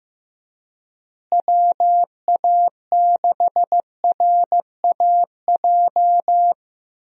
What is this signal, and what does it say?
Morse code, heard as a keyed tone, about 15 wpm, 700 Hz. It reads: WA6RAJ
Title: USA Call Signs - 6 Characters - Repeated 2x - 15wpm